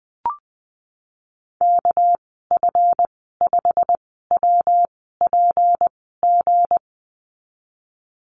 Morse code, heard as a keyed tone, about 20 wpm, spaced as for 15 wpm, 700 Hz. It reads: KF5WPG
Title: 207 Lesson 33g - Letter j - Call Signs 20wpm